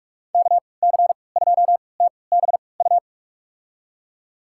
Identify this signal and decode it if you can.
Morse code, heard as a keyed tone, about 47 wpm, spaced as for 26 wpm, 700 Hz. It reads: KC2TBU
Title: USA Call Signs - 6 Characters - 3x Character Spacing - 45wpm